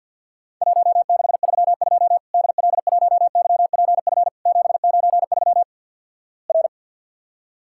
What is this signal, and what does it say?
Morse code, heard as a keyed tone, about 50 wpm, 700 Hz. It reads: 1632 DL1YPF 793 R
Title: Worked All Europe DX Contest - QTC Exchange with Repeats - 50wpm